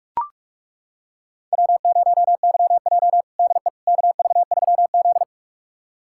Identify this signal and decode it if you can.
Morse code, heard as a keyed tone, about 41 wpm, 700 Hz. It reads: W0YJ DE KV3Z
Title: USA Callsign DE Callsign - 4 Characters - 45wpm